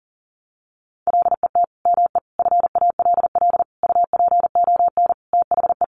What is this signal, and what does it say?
Morse code, heard as a keyed tone, about 40 wpm, 700 Hz. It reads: LET ME FALL UPON THE
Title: Sentences from Top 1100 Words 40wpm